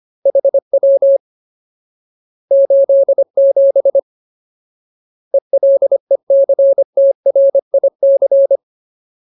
Morse code, heard as a keyed tone, about 25 wpm, 550 Hz. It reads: HW 87 ELECTRIC
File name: Ultimate Word List / 25WPM Ultimate